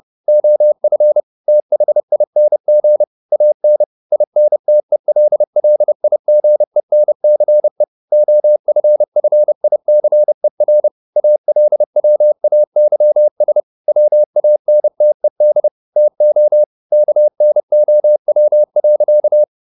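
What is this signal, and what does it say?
Morse code, heard as a keyed tone, about 30 wpm, 600 Hz. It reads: OF THING AN INTELLIGENCE OFFICER ALWAYS WANTED TO KNOW.
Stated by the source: Morse code at 30 wpm, the novel Never by Ken Follett in English, from CW med Gnister.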